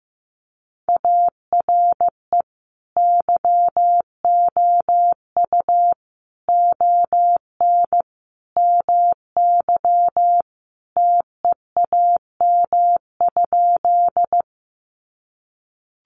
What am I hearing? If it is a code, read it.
Morse code, heard as a keyed tone, about 15 wpm, 700 Hz. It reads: ARE YOU ON MY TEAM?
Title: Sentences from Top 1500 Words 15wpm